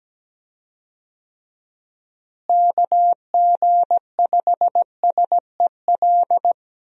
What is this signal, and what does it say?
Morse code, heard as a keyed tone, about 17 wpm, 700 Hz. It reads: KG5SEL
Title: USA Call Signs - 6 Characters - Repeated 3x - 17wpm